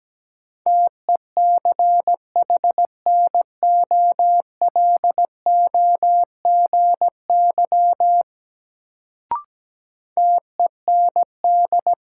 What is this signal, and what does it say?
Morse code, heard as a keyed tone, about 17 wpm, 700 Hz. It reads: TECHNOLOGY TEND
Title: Sentences from Top 1000 Words - Review Part 5 - New Word Drill - 17wpm